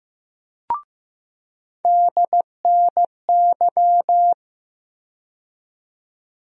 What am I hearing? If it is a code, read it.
Morse code, heard as a keyed tone, about 15 wpm, 700 Hz. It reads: DNY
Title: USA Call Sign Postfixes - 3 Letters 15wpm